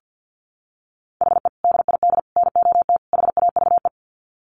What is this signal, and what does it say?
Morse code, heard as a keyed tone, about 49 wpm, 700 Hz. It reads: HE DID NOT HAVE
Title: Sets of 4 Words 50wpm